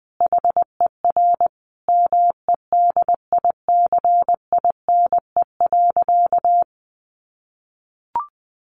Morse code, heard as a keyed tone, about 20 wpm, 700 Hz. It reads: HER MEDICINE.
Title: Sentences from Top 900 Words 20wpm